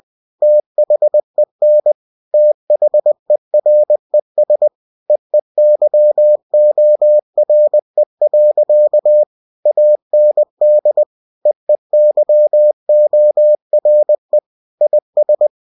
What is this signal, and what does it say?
Morse code, heard as a keyed tone, about 20 wpm, 600 Hz. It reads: THEN THERES EEYORE. AND EEYORE IS
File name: cw-books / winnie-the-pooh / winnie-20wpm-page-0092